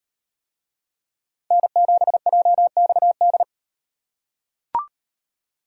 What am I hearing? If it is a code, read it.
Morse code, heard as a keyed tone, about 38 wpm, 700 Hz. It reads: N7JXD